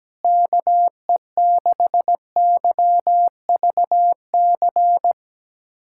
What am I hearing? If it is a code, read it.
Morse code, heard as a keyed tone, about 17 wpm, 700 Hz. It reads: KE6YVC
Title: USA Callsign DE Callsign - 6 Characters - 17wpm